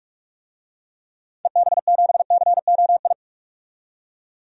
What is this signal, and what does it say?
Morse code, heard as a keyed tone, about 41 wpm, 700 Hz. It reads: EB7COI
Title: DXCC-100 Call Signs - 6 Characters - 45wpm